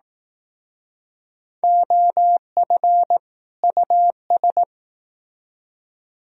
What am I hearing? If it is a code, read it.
Morse code, heard as a keyed tone, about 18 wpm, 700 Hz. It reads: OF US
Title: Speed-Racing - Sets of 2 Words 15wpm